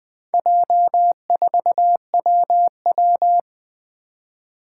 Morse code, heard as a keyed tone, about 20 wpm, 700 Hz. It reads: J4WW
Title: Four Challenging Letter-Number Combinations 20wpm